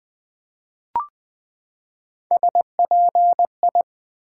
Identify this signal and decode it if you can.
Morse code, heard as a keyed tone, about 20 wpm, 700 Hz. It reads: SPI
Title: Build A Word - 3 Letters At A Time - 6-Letter Words - 20wpm